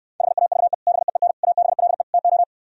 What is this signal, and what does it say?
Morse code, heard as a keyed tone, about 67 wpm, 700 Hz. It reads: HAVE BEEN ABLE TO